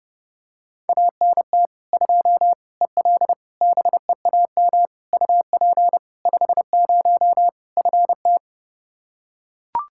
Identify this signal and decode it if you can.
Morse code, heard as a keyed tone, about 32 wpm, 700 Hz. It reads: ANT 2 EL BEAM UP 50 FT
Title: Practice QSO - Exchange 2 - Info - 30wpm